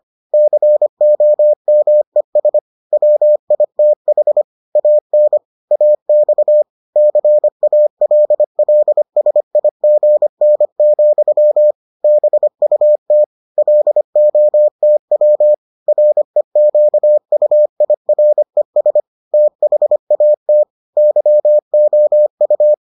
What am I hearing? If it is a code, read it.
Morse code, heard as a keyed tone, about 25 wpm, 600 Hz. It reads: COMES WITH AN AX CALLSIGN, BUT LOTW REQUIRES THAT YOU